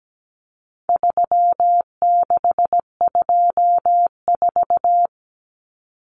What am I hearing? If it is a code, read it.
Morse code, heard as a keyed tone, about 17 wpm, 700 Hz. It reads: 3624
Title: Common Cut Numbers - 4-Digit 17wpm